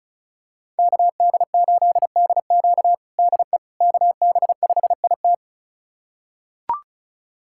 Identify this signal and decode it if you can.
Morse code, heard as a keyed tone, about 34 wpm, 700 Hz. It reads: KD8DQ DE KB5IT
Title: USA Callsign DE Callsign - 5 Characters - 35wpm